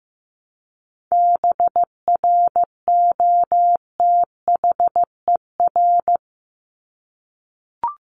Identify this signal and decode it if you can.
Morse code, heard as a keyed tone, about 15 wpm, 700 Hz. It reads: BROTHER
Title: Top 1751-2000 Words 15wpm